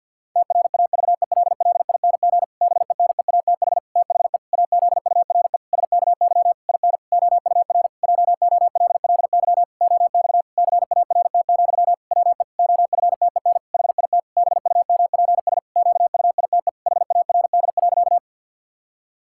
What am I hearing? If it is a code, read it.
Morse code, heard as a keyed tone, about 52 wpm, 700 Hz. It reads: TRAVELLING BENEATH THE AZURE SKY IN OUR JOLLY OX CART, WE OFTEN HIT BUMPS QUITE HARD.
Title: Pangram Challenge v2 - 50wpm